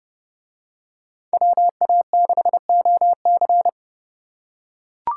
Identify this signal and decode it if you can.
Morse code, heard as a keyed tone, about 28 wpm, 700 Hz. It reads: WA6OC